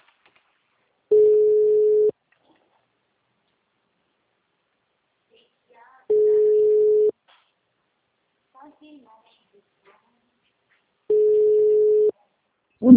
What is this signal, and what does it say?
Ringback tone (Europe, 425 Hz), with speech.
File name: out- -2041- - - .16280